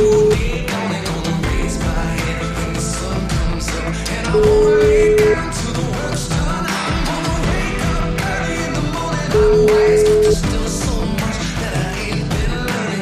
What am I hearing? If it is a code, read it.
Ringback tone (Europe, 425 Hz), with noise.